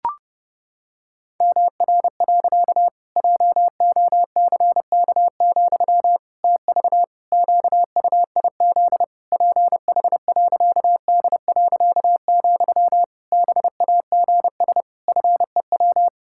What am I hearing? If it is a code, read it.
Morse code, heard as a keyed tone, about 30 wpm, 700 Hz. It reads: MR. JOCK, TV QUIZ PH.D., BAGS FEW
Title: Pangram Challenge v2 - 30wpm